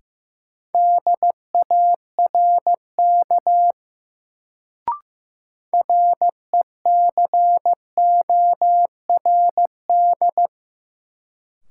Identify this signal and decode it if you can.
Morse code, heard as a keyed tone, about 15 wpm, 700 Hz. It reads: DARK RECORD
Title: Sentences from Top 500 Words - New Word Drill - 15wpm